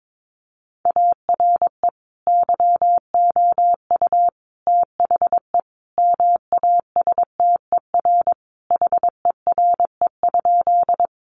Morse code, heard as a keyed tone, about 22 wpm, 700 Hz. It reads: ARE YOU THE MASTER HERE?
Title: Sentences from Top 1300 Words 22wpm